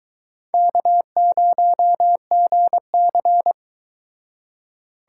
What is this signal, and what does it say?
Morse code, heard as a keyed tone, about 24 wpm, 700 Hz. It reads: K0GC